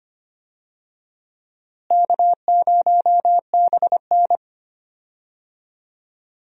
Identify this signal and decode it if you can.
Morse code, heard as a keyed tone, about 25 wpm, 700 Hz. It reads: K0BN
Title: USA Call Signs - 4 Characters - Repeated 3x - 25wpm